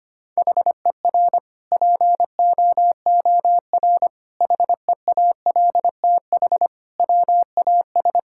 Morse code, heard as a keyed tone, about 25 wpm, 700 Hz. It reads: HER POOR HEALTH WAS
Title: Sentences from Top 1000 Words - Review Part 6 - 25wpm